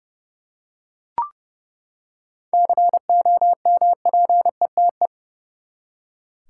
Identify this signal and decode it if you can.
Morse code, heard as a keyed tone, about 30 wpm, 700 Hz. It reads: COMPETE
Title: Seven Letter Words - 30wpm